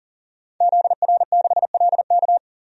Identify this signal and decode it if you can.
Morse code, heard as a keyed tone, about 37 wpm, 700 Hz. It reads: ZR6LK
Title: DXCC-100 Call Signs - 5 Characters - 40wpm